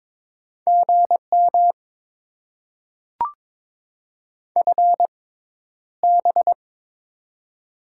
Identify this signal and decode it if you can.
Morse code, heard as a keyed tone, about 22 wpm, spaced as for 11 wpm, 690 Hz. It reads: GM FB
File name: Top 20 QSO Elements - 6x Character Spacing - 22wpm